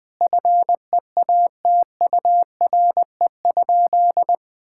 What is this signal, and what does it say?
Morse code, heard as a keyed tone, about 20 wpm, 700 Hz. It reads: FEATURE?
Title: Sentences from Top 1000 Words - Review Part 5 - 20wpm